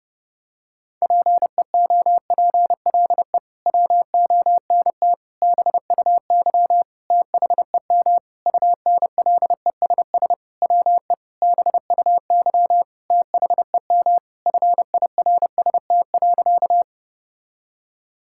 Morse code, heard as a keyed tone, about 30 wpm, 700 Hz. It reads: PEOPLE WONT BUY THEM UNLESS WE BUY THEM FIRST.
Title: Sentences from Top 1000 Words - Review Part 6 - 30wpm